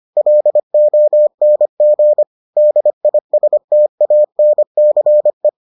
Morse code, heard as a keyed tone, about 25 wpm, 600 Hz. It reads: LONG DISTANCE